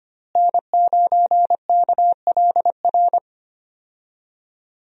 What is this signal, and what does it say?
Morse code, heard as a keyed tone, about 25 wpm, 700 Hz. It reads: N9KLR
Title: USA Call Signs - 5 Characters - Repeated 3x - 25wpm